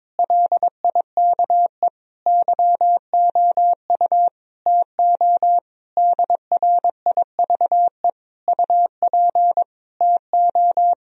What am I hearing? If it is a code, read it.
Morse code, heard as a keyed tone, about 22 wpm, 700 Hz. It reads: LIKE YOU TO DRIVE UP TO
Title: Sentences from Top 1000 Words - Review Part 2 - 22wpm